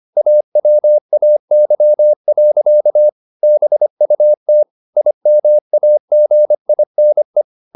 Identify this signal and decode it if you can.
Morse code, heard as a keyed tone, about 25 wpm, 600 Hz. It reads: AWAY. BUT IMAGINE